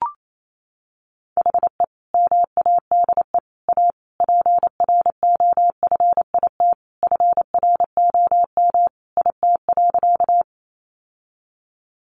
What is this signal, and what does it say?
Morse code, heard as a keyed tone, about 28 wpm, 700 Hz. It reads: HE MADE A PROFIT FROM IT.
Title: Sentences from Top 800 Words 28wpm